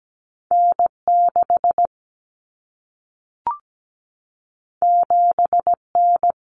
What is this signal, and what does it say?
Morse code, heard as a keyed tone, about 17 wpm, 700 Hz. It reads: N6 7N
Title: Common Cut Numbers - 2-Digit 17wpm